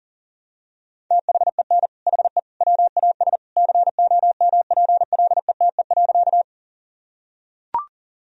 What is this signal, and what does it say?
Morse code, heard as a keyed tone, about 40 wpm, 700 Hz. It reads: THEN HE WAS COMPLETE.
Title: Sentences from Top 1000 Words - Review Part 2 - 40wpm